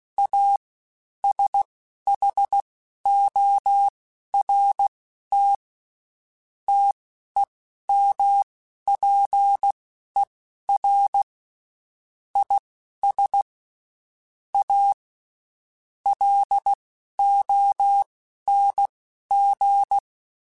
Morse code, heard as a keyed tone, about 16 wpm, spaced as for 12 wpm, 800 Hz. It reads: A SHORT TEMPER IS A LONG
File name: Quotes Spaced / Quotes Spaced-mp3s / Temper sp @15